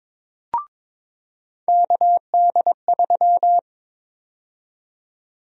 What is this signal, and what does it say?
Morse code, heard as a keyed tone, about 22 wpm, 700 Hz. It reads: KD3
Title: USA Call Sign Prefixes - 22wpm